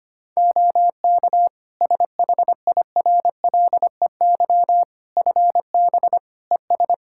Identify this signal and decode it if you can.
Morse code, heard as a keyed tone, about 25 wpm, 700 Hz. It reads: OK SHIRLEY FB ES
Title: Practice QSO - Exchange 2 - Info - 25wpm